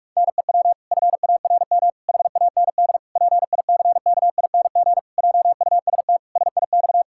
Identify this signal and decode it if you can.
Morse code, heard as a keyed tone, about 41 wpm, 700 Hz. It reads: NEW FARM HAND PICKING JUST SIX
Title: Pangram Challenge v2 - 45wpm